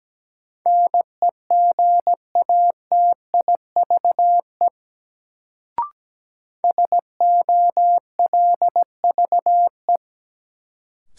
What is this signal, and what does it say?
Morse code, heard as a keyed tone, about 17 wpm, 700 Hz. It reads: NEGATIVE SOLVE
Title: Sentences from Top 1000 Words - New Word Drill - 17wpm